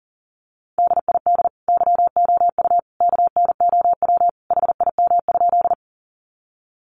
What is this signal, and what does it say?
Morse code, heard as a keyed tone, about 40 wpm, 700 Hz. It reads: DID YOU KNOW HIM?
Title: Sentences from Top 100 Words 40wpm